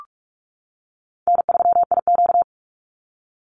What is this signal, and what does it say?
Morse code, heard as a keyed tone, about 44 wpm, 700 Hz. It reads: N3IQ
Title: Call Signs - 45wpm